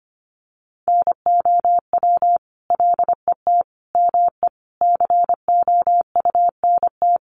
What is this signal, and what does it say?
Morse code, heard as a keyed tone, about 25 wpm, 700 Hz. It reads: NOW LET ME COUNT
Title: Sentences from Top 1500 Words 25wpm